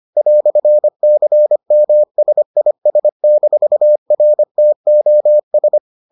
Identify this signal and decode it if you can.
Morse code, heard as a keyed tone, about 25 wpm, 600 Hz. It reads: "CMSIS-RTOS